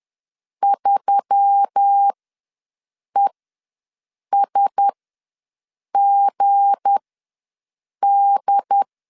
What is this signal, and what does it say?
Morse code, heard as a keyed tone, about 10 wpm, spaced as for 6 wpm, 800 Hz. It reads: 3ESGD